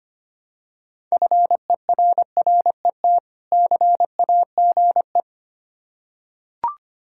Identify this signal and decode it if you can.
Morse code, heard as a keyed tone, about 25 wpm, 700 Hz. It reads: FERRET CAGE
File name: practice / 25wpm / Difficult Words with Hints - 25wpm